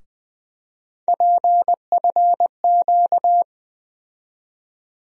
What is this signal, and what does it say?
Morse code, heard as a keyed tone, about 20 wpm, 700 Hz. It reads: PFQ
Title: USA Call Sign Postfixes - 3 Letters 20wpm